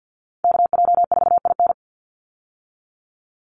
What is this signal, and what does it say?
Morse code, heard as a keyed tone, about 50 wpm, 700 Hz. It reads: KJ4IN
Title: USA Call Signs 50wpm